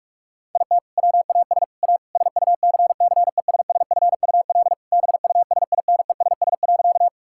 Morse code, heard as a keyed tone, about 41 wpm, 700 Hz. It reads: IT WAS A SUCCESSFUL BUSINESS.
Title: Sentences from Top 1000 Words - Review Part 6 - 45wpm